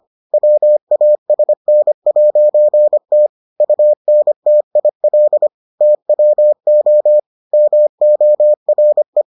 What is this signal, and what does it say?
Morse code, heard as a keyed tone, about 25 wpm, 600 Hz. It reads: WASN'T UNTIL TWO MORE